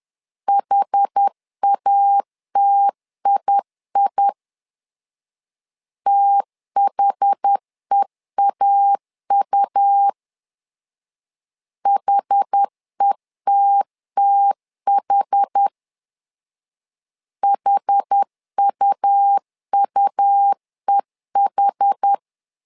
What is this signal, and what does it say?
Morse code, heard as a keyed tone, about 10 wpm, 800 Hz. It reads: HATII THEAU HETTH HUUEH
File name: LEZIONI / 2 UHDN / 2 UH 10wpm 1714